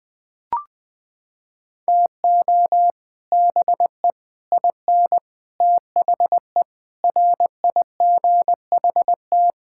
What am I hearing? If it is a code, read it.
Morse code, heard as a keyed tone, about 20 wpm, 700 Hz. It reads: TO BE IN THE RIGHT
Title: Sets of 9 Words - 20wpm